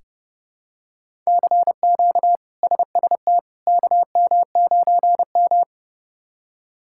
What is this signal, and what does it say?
Morse code, heard as a keyed tone, about 30 wpm, 700 Hz. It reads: CQ SST KM9M
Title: K1USN SST QSO Practice - 30wpm